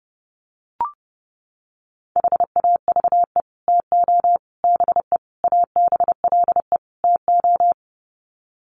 Morse code, heard as a keyed tone, about 30 wpm, 700 Hz. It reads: HAVE TO BE ABLE TO
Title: Sets of 5 Words 30wpm